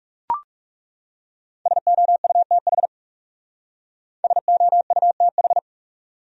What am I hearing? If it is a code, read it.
Morse code, heard as a keyed tone, about 36 wpm, 700 Hz. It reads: SOUTH SOUTH
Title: Speed-Racing Top 500 Words 30wpm